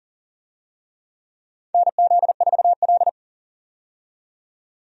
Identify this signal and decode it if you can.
Morse code, heard as a keyed tone, about 40 wpm, 700 Hz. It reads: NZ4L
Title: USA Call Signs - 4 Characters - 40wpm